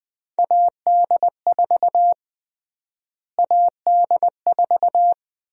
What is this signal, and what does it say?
Morse code, heard as a keyed tone, about 20 wpm, 700 Hz. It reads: AD4 AD4
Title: USA Call Sign Prefixes - Repeated 2x - 20wpm